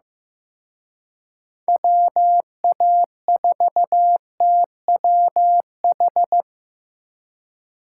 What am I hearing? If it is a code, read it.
Morse code, heard as a keyed tone, about 18 wpm, 700 Hz. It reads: WA4TWH
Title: Speed Racing - USA Call Signs 15wpm